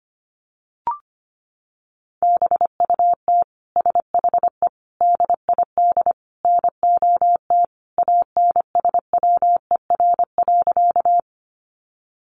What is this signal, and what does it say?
Morse code, heard as a keyed tone, about 25 wpm, 700 Hz. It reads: BUT SHE DID NOT ANSWER.
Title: Sentences from Top 600 Words 25wpm